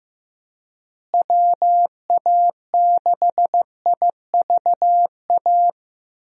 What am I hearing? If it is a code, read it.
Morse code, heard as a keyed tone, about 15 wpm, 700 Hz. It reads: WA6IVA
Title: USA Call Signs - 6 Characters - Repeated 3x - 15wpm